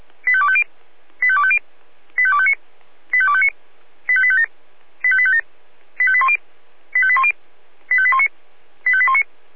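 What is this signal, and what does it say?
Five-tone selective call (ZVEI), heard as quick runs of five tones, tones of about 70 ms. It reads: 85250 85250 85259 85259 85857 85857 85810 85810 85819 85819
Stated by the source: ZVEI five-tone selective call 85253, received by SDR.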